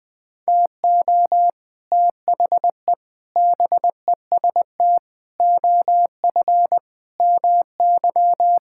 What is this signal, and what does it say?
Morse code, heard as a keyed tone, about 21 wpm, 700 Hz. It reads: TO THE BEST OF MY
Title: Sentences with 6-Word Phrases - 20wpm